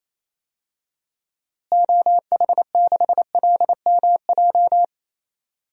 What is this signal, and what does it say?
Morse code, heard as a keyed tone, about 28 wpm, 700 Hz. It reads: OH6LMJ
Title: DXCC-100 Call Signs - 6 Characters - Repeated 3x - 28wpm